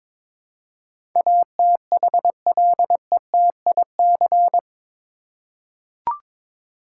Morse code, heard as a keyed tone, about 22 wpm, 700 Hz. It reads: ATHLETIC
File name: practice / 22wpm / Eight Letter Words - 22wpm